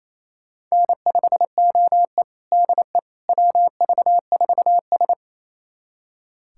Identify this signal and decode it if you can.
Morse code, heard as a keyed tone, about 28 wpm, 700 Hz. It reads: N5OE DE WV4S
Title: USA Callsign DE Callsign - 4 Characters - 28wpm